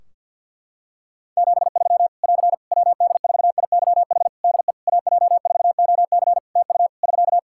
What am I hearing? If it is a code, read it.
Morse code, heard as a keyed tone, about 50 wpm, 700 Hz. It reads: 73 <AR> WD4IYH DE AJ4OC TU <SK>
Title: Practice QSO - Exchange 4 - Ending - 50wpm